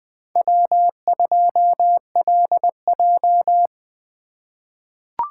Morse code, heard as a keyed tone, about 20 wpm, 700 Hz. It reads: W2LJ
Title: USA Call Signs - 4 Characters - 4x Character Spacing - 20wpm